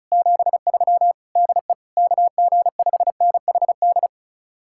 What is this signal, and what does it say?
Morse code, heard as a keyed tone, about 45 wpm, 700 Hz. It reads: 73 DE KG5NHD